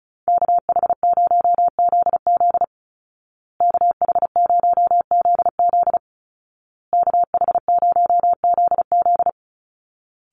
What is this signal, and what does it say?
Morse code, heard as a keyed tone, about 34 wpm, 700 Hz. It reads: KH0ZZ KH0ZZ KH0ZZ
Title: Call Signs Repeated 2x - 35wpm